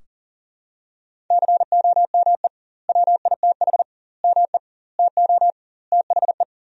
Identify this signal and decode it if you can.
Morse code, heard as a keyed tone, about 40 wpm, 700 Hz. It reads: COME WITH ME TO THE
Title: 2x Word-Spacing - Sentences from Top 200 Words - 40wpm